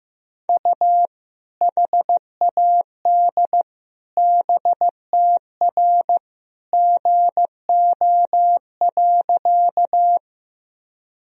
Morse code, heard as a keyed tone, about 15 wpm, 700 Hz. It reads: U HAD BTR GO.
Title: Sentences from Top 400 Words and Abbreviations 15wpm